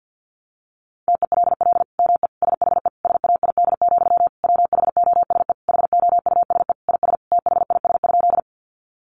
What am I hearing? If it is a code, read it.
Morse code, heard as a keyed tone, about 50 wpm, 700 Hz. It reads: TELL ME SHE SAID, WHOSE HOUSE IS THIS?
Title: Sentences from Top 1200 Words 50wpm